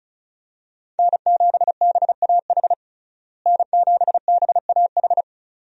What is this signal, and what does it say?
Morse code, heard as a keyed tone, about 34 wpm, 700 Hz. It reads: N7BAH N7BAH
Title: USA Call Signs - 5 Characters - Repeated 2x - 35wpm